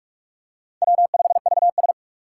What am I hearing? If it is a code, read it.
Morse code, heard as a keyed tone, about 42 wpm, 700 Hz. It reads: W5VS
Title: USA Call Signs - 4 Characters - 6x Character Spacing - 45wpm